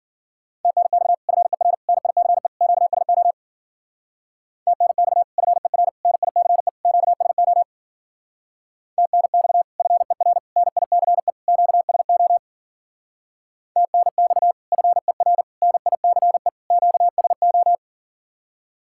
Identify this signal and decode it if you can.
Morse code, heard as a keyed tone, about 52 wpm, 700 Hz. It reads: TNX FER NICE QSO TNX FER NICE QSO TNX FER NICE QSO TNX FER NICE QSO